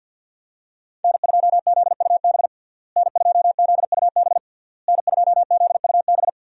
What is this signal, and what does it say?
Morse code, heard as a keyed tone, about 48 wpm, 700 Hz. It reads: N2ZUB N2ZUB N2ZUB
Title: USA Call Signs - 5 Characters - Repeated 3x - 50wpm